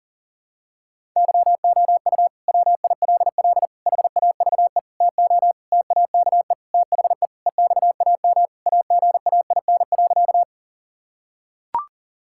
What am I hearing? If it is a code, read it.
Morse code, heard as a keyed tone, about 40 wpm, 700 Hz. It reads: YOU WILL HAVE TO TAKE THE EXAM AGAIN.
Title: Sentences from Top 1000 Words - Review Part 5 - 40wpm